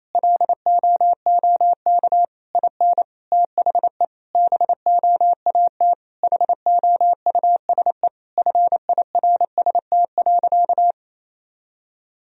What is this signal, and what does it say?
Morse code, heard as a keyed tone, about 27 wpm, 700 Hz. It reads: LOOK IN THE BOAT HOUSE FIRST.
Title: Sentences from Top 1000 Words - Review Part 3 - 28wpm